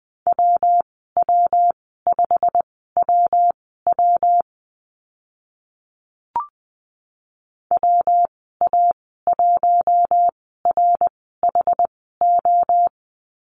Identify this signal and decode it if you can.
Morse code, heard as a keyed tone, about 20 wpm, spaced as for 14 wpm, 700 Hz. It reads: WW5WW WA1RHO
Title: 128 Lesson 21g - Number 6 - Call Signs 20wpm